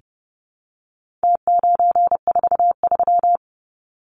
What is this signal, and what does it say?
Morse code, heard as a keyed tone, about 30 wpm, 700 Hz. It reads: T943
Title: Numbers - Contest Style - 4-Digit 30wpm